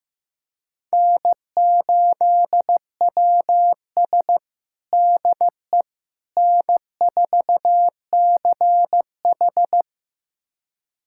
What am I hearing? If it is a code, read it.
Morse code, heard as a keyed tone, about 15 wpm, 700 Hz. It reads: N8WS DE N4CH
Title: USA Callsign DE Callsign - 4 Characters - 15wpm